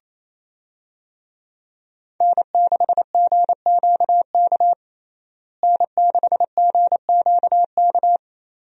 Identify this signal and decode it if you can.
Morse code, heard as a keyed tone, about 28 wpm, 700 Hz. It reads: N6GQK N6GQK
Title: USA Call Signs - 5 Characters - Repeated 2x - 28wpm